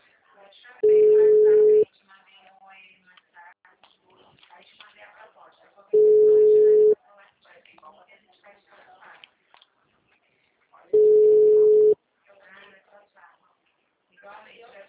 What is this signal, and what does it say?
Ringback tone (Europe, 425 Hz), with speech.